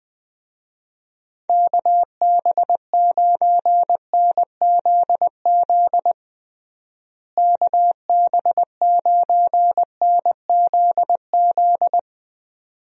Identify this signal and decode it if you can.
Morse code, heard as a keyed tone, about 20 wpm, 700 Hz. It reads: KB9NZZ KB9NZZ
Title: USA Call Signs - 6 Characters - Repeated 2x - 20wpm